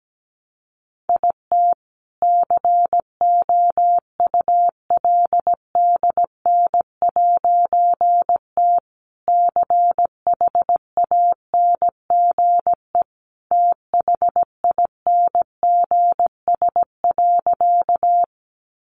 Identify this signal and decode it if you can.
Morse code, heard as a keyed tone, about 17 wpm, 700 Hz. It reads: IT COULDN'T CHANGE THINGS.
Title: A Sound Of Thunder 17wpm